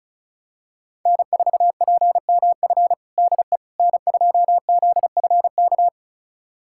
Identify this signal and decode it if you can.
Morse code, heard as a keyed tone, about 36 wpm, 700 Hz. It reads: N4PMF DE N2ZFK
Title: USA Callsign DE Callsign - 5 Characters - 35wpm